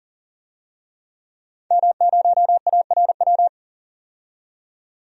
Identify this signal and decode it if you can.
Morse code, heard as a keyed tone, about 40 wpm, 700 Hz. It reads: M0ARW
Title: DXCC-100 Call Signs - 5 Characters - 40wpm